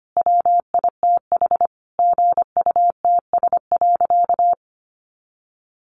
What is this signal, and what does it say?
Morse code, heard as a keyed tone, about 25 wpm, 700 Hz. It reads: WITH GUTS.
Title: A Sound Of Thunder 25wpm